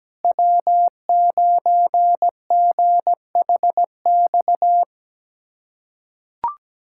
Morse code, heard as a keyed tone, about 17 wpm, 700 Hz. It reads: W9GHX
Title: K1USN SST QSO Practice - 17wpm